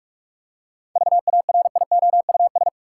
Morse code, heard as a keyed tone, about 46 wpm, 700 Hz. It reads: VARIOUS
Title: Seven Letter Words - 45wpm